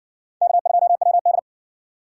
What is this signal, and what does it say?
Morse code, heard as a keyed tone, about 54 wpm, 700 Hz. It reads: K2WD